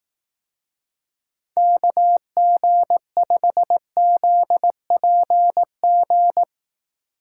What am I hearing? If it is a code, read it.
Morse code, heard as a keyed tone, about 18 wpm, 700 Hz. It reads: KG5ZPG
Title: Speed Racing - USA 2x3 Call Signs 15wpm